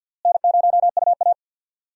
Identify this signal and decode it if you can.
Morse code, heard as a keyed tone, about 50 wpm, 700 Hz. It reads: N0UA
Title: USA Call Signs - 4 Characters - Repeated 3x - 50wpm